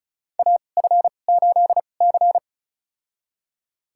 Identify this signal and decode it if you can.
Morse code, heard as a keyed tone, about 34 wpm, spaced as for 25 wpm, 700 Hz. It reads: AF8C
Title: USA Call Signs - 4 Characters - 2x Character Spacing - 35wpm